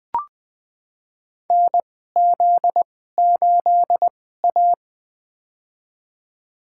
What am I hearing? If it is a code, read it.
Morse code, heard as a keyed tone, about 20 wpm, spaced as for 14 wpm, 700 Hz. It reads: NZ8A
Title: USA Call Signs - 4 Characters - 2x Character Spacing - 20wpm